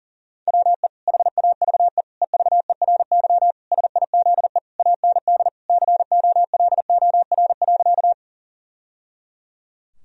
Morse code, heard as a keyed tone, about 40 wpm, 700 Hz. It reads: WE HAVE EVERY SIZE AND COLOR.
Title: Sentences from Top 1000 Words - Review Part 2 - 40wpm